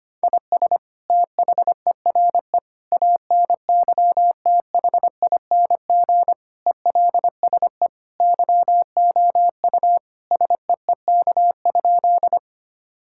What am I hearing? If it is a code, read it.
Morse code, heard as a keyed tone, about 25 wpm, 700 Hz. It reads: IS THERE ANYTHING ELSE YOU SEEK?
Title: Sentences from Top 1600 Words 25wpm